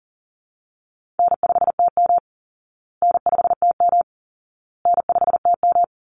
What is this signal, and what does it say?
Morse code, heard as a keyed tone, about 40 wpm, 700 Hz. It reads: N5TM N5TM N5TM
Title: Easy Call Signs Repeated 3x - 40wpm